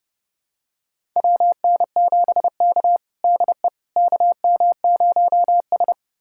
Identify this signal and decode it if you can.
Morse code, heard as a keyed tone, about 32 wpm, 700 Hz. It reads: WN7K DE KM0S
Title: USA Callsign DE Callsign - 4 Characters - 30wpm